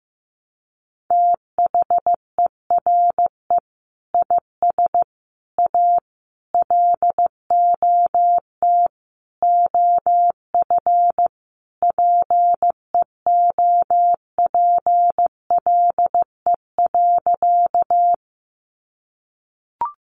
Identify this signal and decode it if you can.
Morse code, heard as a keyed tone, about 15 wpm, 700 Hz. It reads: THERE IS A LOT OF PEOPLE.
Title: Sentences from Top 500 Words 15wpm